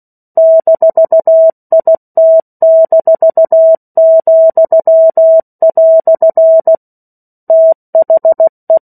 Morse code, heard as a keyed tone, about 16 wpm, 650 Hz. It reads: -IT-," THE
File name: 03-Chapter-03 16 WPM